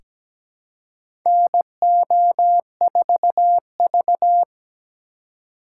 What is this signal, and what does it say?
Morse code, heard as a keyed tone, about 17 wpm, 700 Hz. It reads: NO4V
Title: USA Call Signs - 4x Character Spacing - 17wpm